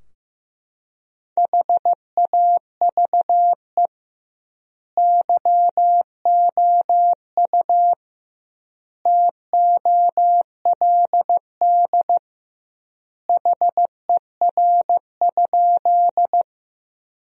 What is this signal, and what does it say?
Morse code, heard as a keyed tone, about 15 wpm, 700 Hz. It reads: HAVE YOU TOLD HER?
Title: 2x Word-Spacing - Sentences from Top 400 Words - 15wpm